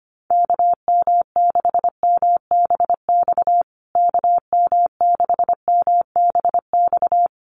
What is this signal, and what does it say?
Morse code, heard as a keyed tone, about 25 wpm, 700 Hz. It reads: KM6MBX KM6MBX
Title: USA Call Signs Repeated 3x 25wpm